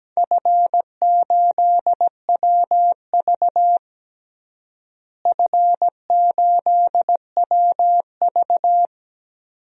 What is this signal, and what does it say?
Morse code, heard as a keyed tone, about 17 wpm, 700 Hz. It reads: F8WV F8WV
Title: DXCC-100 Call Signs - 4 Characters - Repeated 2x - 17wpm